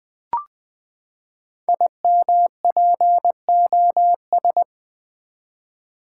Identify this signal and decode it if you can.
Morse code, heard as a keyed tone, about 20 wpm, 700 Hz. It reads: IMPOS
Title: Build A Word - 5 Letters At A Time - 10-Letter Words - 20wpm